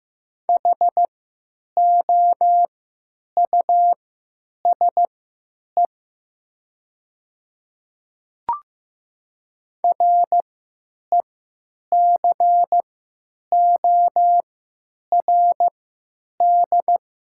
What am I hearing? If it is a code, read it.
Morse code, heard as a keyed tone, about 15 wpm, spaced as for 9 wpm, 700 Hz. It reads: HOUSE RECORD
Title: 3x Character Spacing - Top 500 Words - 15wpm